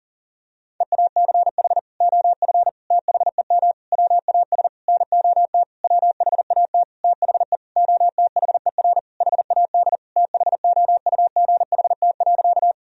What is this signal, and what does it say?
Morse code, heard as a keyed tone, about 37 wpm, 700 Hz. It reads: EACH OF THEM WAS NOT WHAT THE OTHER HAD THOUGHT.
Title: Sentences from Top 1000 Words - Review Part 1 - 40wpm